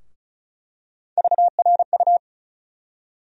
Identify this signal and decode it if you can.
Morse code, heard as a keyed tone, about 36 wpm, 700 Hz. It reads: VRU
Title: USA Call Sign Postfixes - 3 Letters 35wpm